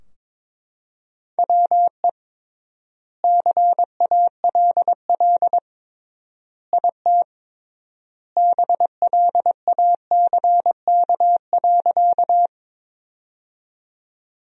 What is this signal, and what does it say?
Morse code, heard as a keyed tone, about 22 wpm, 700 Hz. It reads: WE CALL IT BLACK.
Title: 3x Word-Spacing - Sentences from Top 400 Words - 22wpm